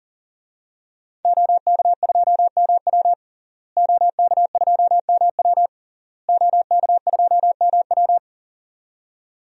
Morse code, heard as a keyed tone, about 40 wpm, 700 Hz. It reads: OK2MW OK2MW OK2MW
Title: DXCC-100 Call Signs - 5 Characters - Repeated 3x - 40wpm